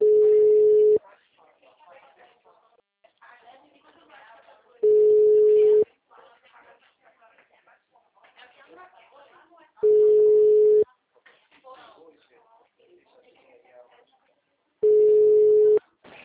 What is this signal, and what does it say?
Ringback tone (Europe, 425 Hz), with speech.